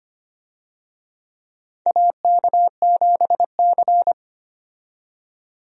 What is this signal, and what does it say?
Morse code, heard as a keyed tone, about 25 wpm, 700 Hz. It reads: AK7C
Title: USA Call Signs - 4 Characters - 25wpm